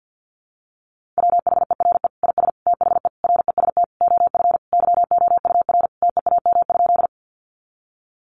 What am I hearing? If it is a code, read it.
Morse code, heard as a keyed tone, about 50 wpm, 700 Hz. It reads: WHERE IS THE REST OF YOUR TEAM?
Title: Sentences from Top 1500 Words 50wpm